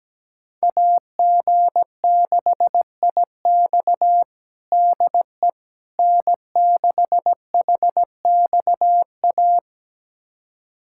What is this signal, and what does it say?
Morse code, heard as a keyed tone, about 17 wpm, 700 Hz. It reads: AG6IX DE N6HXA
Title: USA Callsign DE Callsign - 5 Characters - 17wpm